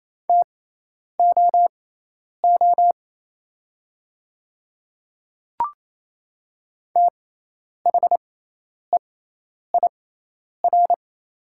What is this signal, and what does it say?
Morse code, heard as a keyed tone, about 27 wpm, spaced as for 10 wpm, 700 Hz. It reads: TOO THEIR
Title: 6x Character Spacing - Top 200 Words - 28wpm